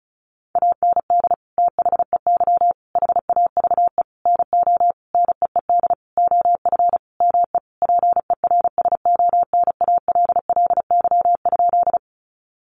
Morse code, heard as a keyed tone, about 34 wpm, 700 Hz. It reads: AND THEY HAVE NO NEED OF ME PERSONALLY?
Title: Sentences from Top 1400 Words 35wpm